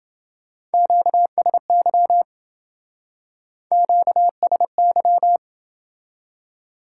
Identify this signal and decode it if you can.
Morse code, heard as a keyed tone, about 28 wpm, 700 Hz. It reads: QSY QSY